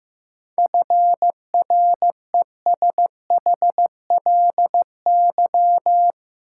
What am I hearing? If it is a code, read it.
Morse code, heard as a keyed tone, about 15 wpm, 700 Hz. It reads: FRESHLY
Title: Sets of 5 Words Encore - New Word Drill - 15wpm